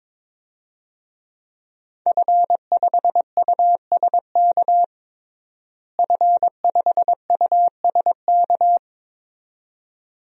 Morse code, heard as a keyed tone, about 22 wpm, 700 Hz. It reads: F5USK F5USK
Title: DXCC-100 Call Signs - 5 Characters - Repeated 2x - 22wpm